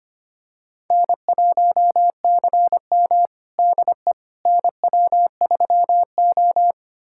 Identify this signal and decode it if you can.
Morse code, heard as a keyed tone, about 25 wpm, 700 Hz. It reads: N1CM DE NW3O
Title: USA Callsign DE Callsign - 4 Characters - 25wpm